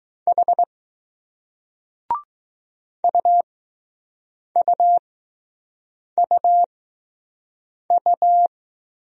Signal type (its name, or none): Morse code